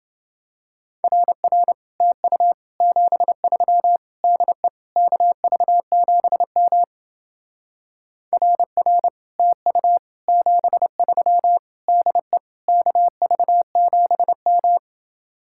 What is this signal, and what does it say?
Morse code, heard as a keyed tone, about 30 wpm, 700 Hz. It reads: RR TU 73 DE KV7M RR TU 73 DE KV7M